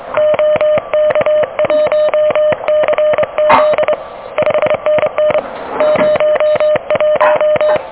Morse code, heard as a keyed tone, about 24 wpm, 600 Hz. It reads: OK1CZ 5NN 01
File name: OK QRP 2023 / OK QRP 2023 OK1DWF